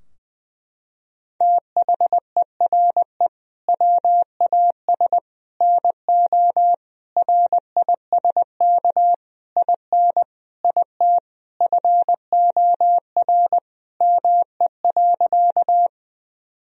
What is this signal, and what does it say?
Morse code, heard as a keyed tone, about 21 wpm, 700 Hz. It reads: THERE WAS NO RISK IN IT FOR ME.
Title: Sentences from Top 1000 Words - Review Part 4 - 20wpm